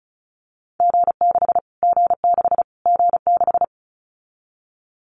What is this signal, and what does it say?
Morse code, heard as a keyed tone, about 34 wpm, 700 Hz. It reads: G6 G6 G6
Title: Callsign Prefixes for Top 25 DXCC - Repeated 3x - 35wpm